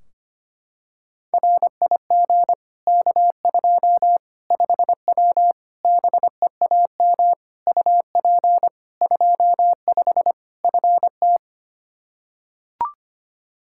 Morse code, heard as a keyed tone, about 25 wpm, 700 Hz. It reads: RIG K2 5W BEAM UP 25 FT
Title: Practice QSO - Exchange 2 - Info - 25wpm